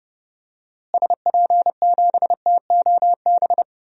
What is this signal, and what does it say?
Morse code, heard as a keyed tone, about 30 wpm, 700 Hz. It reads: SP7TOB
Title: DXCC-100 Call Signs - 6 Characters - Repeated 3x - 30wpm